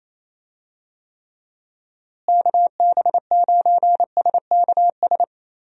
Morse code, heard as a keyed tone, about 27 wpm, 700 Hz. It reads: KB9SKS
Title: USA Call Signs - 6 Characters - Repeated 2x - 28wpm